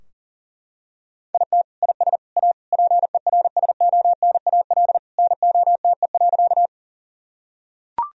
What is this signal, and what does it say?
Morse code, heard as a keyed tone, about 40 wpm, 700 Hz. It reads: IT IS A PERSONAL NOTE.
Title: Sentences from Top 1000 Words - Review Part 4 - 40wpm